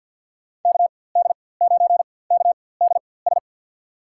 Morse code, heard as a keyed tone, about 49 wpm, spaced as for 23 wpm, 700 Hz. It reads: KD9KDS
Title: USA Call Signs - 4x Character Spacing - 50wpm